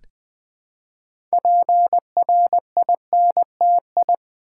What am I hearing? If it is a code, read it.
Morse code, heard as a keyed tone, about 20 wpm, 700 Hz. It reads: PRINTI
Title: Build A Word - 2 Letters At A Time - 8-Letter Words - 20wpm